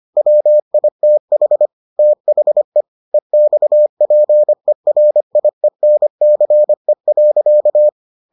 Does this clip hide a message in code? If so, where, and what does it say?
Yes: Morse code from 0.2 to 7.9 s, heard as a keyed tone, about 25 wpm, 600 Hz. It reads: WITH THE EXPERIENCE.